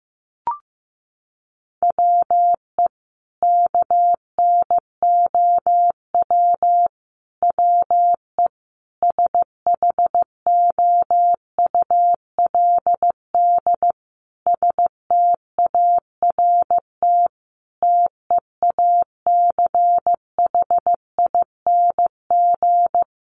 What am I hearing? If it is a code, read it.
Morse code, heard as a keyed tone, about 15 wpm, 700 Hz. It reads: WE KNOW WE SHOULD START TEACHING
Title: Sentences from Top 1100 Words 15wpm